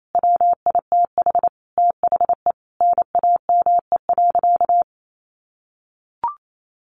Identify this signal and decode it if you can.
Morse code, heard as a keyed tone, about 28 wpm, 700 Hz. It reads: WITH THE NAME.
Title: Sentences from Top 1100 Words 28wpm